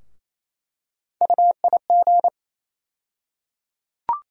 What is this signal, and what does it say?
Morse code, heard as a keyed tone, about 28 wpm, 700 Hz. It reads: UIG
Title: USA Call Sign Postfixes - 3 Letters 28wpm